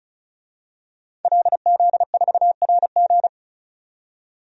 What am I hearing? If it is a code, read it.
Morse code, heard as a keyed tone, about 36 wpm, 700 Hz. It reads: LZ4RG
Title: DXCC-100 Call Signs - 5 Characters - 35wpm